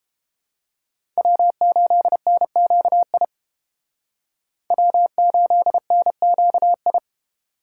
Morse code, heard as a keyed tone, about 30 wpm, 700 Hz. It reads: W8NQI W8NQI